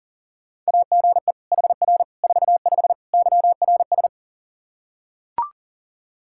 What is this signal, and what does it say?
Morse code, heard as a keyed tone, about 40 wpm, 700 Hz. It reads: AGE HR 45 YRS
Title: Practice QSO - Exchange 3 - More Info - 40wpm